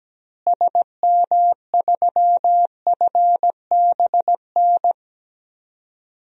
Morse code, heard as a keyed tone, about 17 wpm, 700 Hz. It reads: SM3FBN
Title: DXCC-100 Call Signs - 6 Characters - Repeated 2x - 17wpm